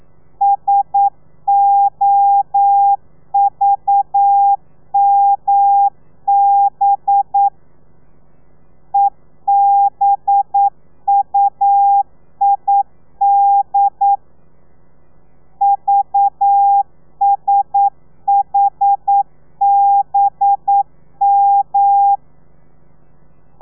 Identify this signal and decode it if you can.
Morse code, heard as a keyed tone, about 9 wpm, 800 Hz. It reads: SOVMB EBUID VSHBM
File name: telegrafia / i2tfj / Lez4 40